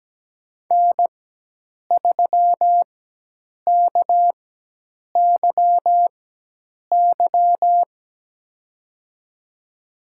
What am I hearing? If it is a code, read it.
Morse code, heard as a keyed tone, about 17 wpm, spaced as for 8 wpm, 700 Hz. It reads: N3KYY
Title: USA Call Signs - 5 Characters - 4x Character Spacing - 17wpm